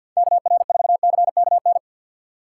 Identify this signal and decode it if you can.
Morse code, heard as a keyed tone, about 52 wpm, 700 Hz. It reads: KR4CKN
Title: USA Call Signs - 3x Character Spacing - 50wpm